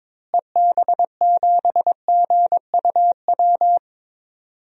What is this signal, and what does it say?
Morse code, heard as a keyed tone, about 21 wpm, 700 Hz. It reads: EB7GUW
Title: DXCC-100 Call Signs - 6 Characters - Repeated 2x - 22wpm